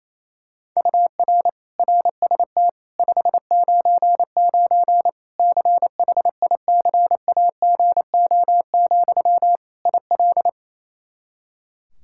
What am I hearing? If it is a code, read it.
Morse code, heard as a keyed tone, about 28 wpm, 700 Hz. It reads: UR RST 599 CHICAGO, IL
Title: Practice QSO - Exchange 1 - Beginning - 28wpm